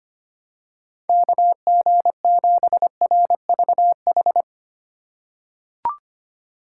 Morse code, heard as a keyed tone, about 25 wpm, 700 Hz. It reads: KG7RVH
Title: USA Call Signs - 6 Characters - 2x Character Spacing - 25wpm